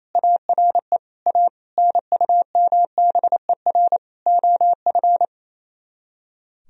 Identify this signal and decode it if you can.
Morse code, heard as a keyed tone, about 36 wpm, 700 Hz. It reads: ARE A NUMBER OF